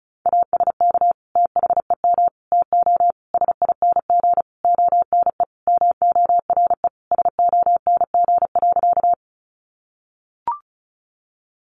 Morse code, heard as a keyed tone, about 34 wpm, 700 Hz. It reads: ASK THEM TO SING ONE MORE SONG.
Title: Sentences from Top 400 Words 35wpm